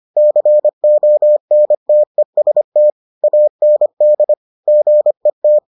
Morse code, heard as a keyed tone, about 25 wpm, 600 Hz. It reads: CONTEST AND GET